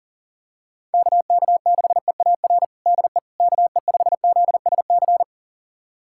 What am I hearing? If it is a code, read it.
Morse code, heard as a keyed tone, about 40 wpm, 700 Hz. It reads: KK6EAR DE KE5ZSC
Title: USA Callsign DE Callsign 40wpm